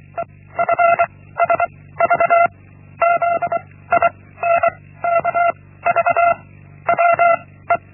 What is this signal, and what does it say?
Morse code, heard as a keyed tone, about 24 wpm, 680 Hz. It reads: E F S V Z I N K V W E
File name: Linksmakalnis-1